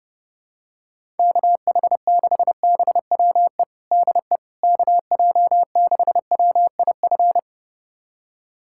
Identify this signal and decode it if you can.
Morse code, heard as a keyed tone, about 30 wpm, 700 Hz. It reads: KH6BWE DE KJ6WIF
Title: USA Callsign DE Callsign 30wpm